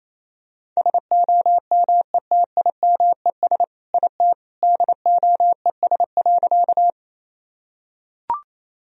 Morse code, heard as a keyed tone, about 28 wpm, 700 Hz. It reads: SOMETIMES IT DOES.
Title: Sentences from Top 1000 Words - Review Part 3 - 28wpm